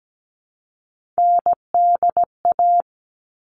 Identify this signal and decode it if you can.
Morse code, heard as a keyed tone, about 17 wpm, 700 Hz. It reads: NDA
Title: All Cut Numbers - 3-Digit 17wpm